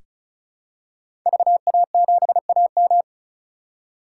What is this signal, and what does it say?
Morse code, heard as a keyed tone, about 36 wpm, 700 Hz. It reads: VA7AM
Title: K1USN SST QSO Practice - 35wpm